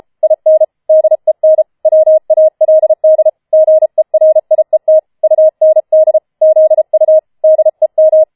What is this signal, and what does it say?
Morse code, heard as a keyed tone, about 33 wpm, 630 Hz. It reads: IN DEN WALD GERIET UND ZU DEM
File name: Schneewittchen Teil5 34wpm